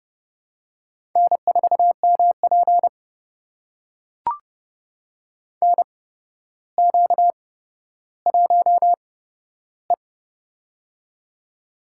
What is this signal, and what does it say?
Morse code, heard as a keyed tone, about 30 wpm, spaced as for 13 wpm, 700 Hz. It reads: N4MP NQ1E
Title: USA Call Signs - 4 Characters - 8x Character Spacing - 30wpm